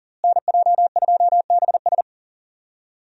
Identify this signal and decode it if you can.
Morse code, heard as a keyed tone, about 39 wpm, 700 Hz. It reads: NJ2BS
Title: USA Call Signs - 5 Characters - Repeated 3x - 40wpm